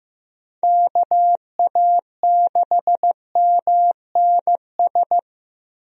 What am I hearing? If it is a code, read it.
Morse code, heard as a keyed tone, about 15 wpm, 700 Hz. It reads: KA6MNS
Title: USA Call Signs - 6 Characters - Repeated 3x - 15wpm